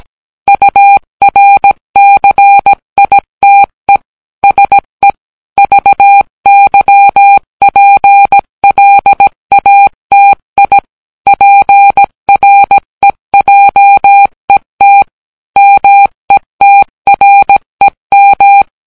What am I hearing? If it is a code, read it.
Morse code, heard as a keyed tone, about 17 wpm, 800 Hz. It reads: URCITE SE VYPLATI PREJET METREM